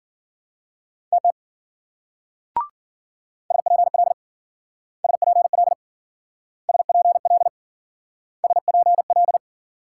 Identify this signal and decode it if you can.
Morse code, heard as a keyed tone, about 52 wpm, 700 Hz. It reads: TT SPL SPL SPL SPL